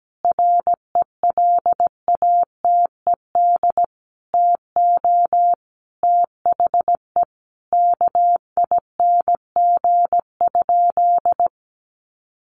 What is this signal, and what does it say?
Morse code, heard as a keyed tone, about 17 wpm, 700 Hz. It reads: RELATED TO THE KING?
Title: Sentences from Top 1400 Words 17wpm